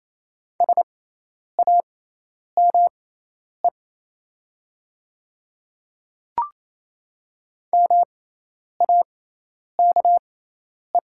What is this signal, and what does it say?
Morse code, heard as a keyed tone, about 28 wpm, spaced as for 10 wpm, 700 Hz. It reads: SAME MAKE
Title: 6x Character Spacing - Top 200 Words - 28wpm